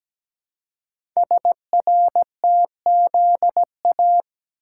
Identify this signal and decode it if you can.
Morse code, heard as a keyed tone, about 17 wpm, 700 Hz. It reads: SRTZA
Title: Code Groups - 5 Letters - 17wpm